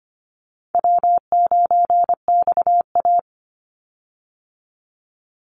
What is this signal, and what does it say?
Morse code, heard as a keyed tone, about 25 wpm, 700 Hz. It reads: W9XA
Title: Easy Call Signs Repeated 3x - 25wpm